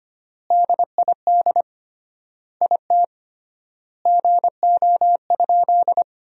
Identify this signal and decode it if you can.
Morse code, heard as a keyed tone, about 25 wpm, 700 Hz. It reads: DID IT GO?
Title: 3x Word-Spacing - Sentences from Top 200 Words - 25wpm